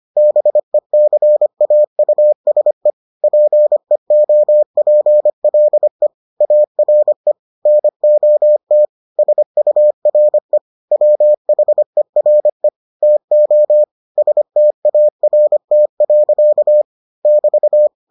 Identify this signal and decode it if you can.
Morse code, heard as a keyed tone, about 25 wpm, 600 Hz. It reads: BECAUSE PEOPLE ARE NOT SURE WHERE TO START. <BT>